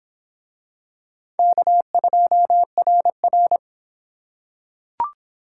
Morse code, heard as a keyed tone, about 25 wpm, 700 Hz. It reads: K2RR